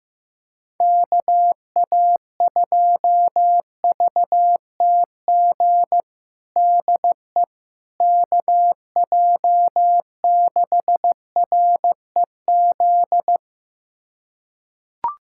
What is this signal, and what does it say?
Morse code, heard as a keyed tone, about 16 wpm, 700 Hz. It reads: KA2VTG DE KJ6REZ
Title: USA Callsign DE Callsign - 6 Characters - 15wpm